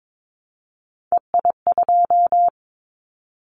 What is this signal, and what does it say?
Morse code, heard as a keyed tone, about 22 wpm, 700 Hz. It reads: EI2
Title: Callsign Prefixes for Top 50 DXCC - 22wpm